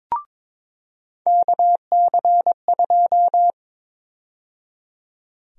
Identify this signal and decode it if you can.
Morse code, heard as a keyed tone, about 22 wpm, 700 Hz. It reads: KC2
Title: USA Call Sign Prefixes - 22wpm